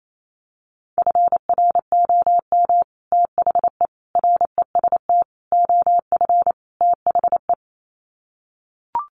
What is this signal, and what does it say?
Morse code, heard as a keyed tone, about 28 wpm, 700 Hz. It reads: FROM THE REST OF THE
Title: Sets of 5 Words 28wpm